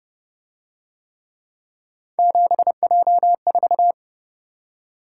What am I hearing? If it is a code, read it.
Morse code, heard as a keyed tone, about 27 wpm, 700 Hz. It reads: 7J4